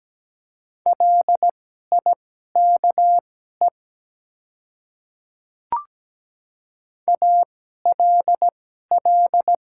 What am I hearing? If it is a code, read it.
Morse code, heard as a keyed tone, about 17 wpm, spaced as for 12 wpm, 700 Hz. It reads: LIKE ALL
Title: 2x Character Spacing - Top 100 Words - 17wpm